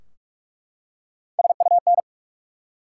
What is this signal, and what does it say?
Morse code, heard as a keyed tone, about 43 wpm, 700 Hz. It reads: SUN
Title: Three Letter Words - 45wpm